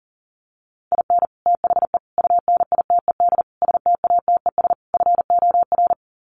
Morse code, heard as a keyed tone, about 40 wpm, 700 Hz. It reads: IN THE UNITED STATES FOR
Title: Sets of 5 Words Encore2 40wpm